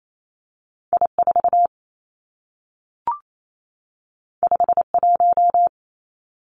Morse code, heard as a keyed tone, about 28 wpm, 700 Hz. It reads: I 4 5 1
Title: Two Challenging Letter-Number Combinations 28wpm